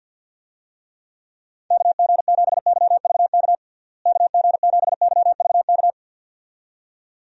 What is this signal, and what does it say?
Morse code, heard as a keyed tone, about 47 wpm, 700 Hz. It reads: KG7YVK KG7YVK
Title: USA Call Signs - 6 Characters - Repeated 2x - 50wpm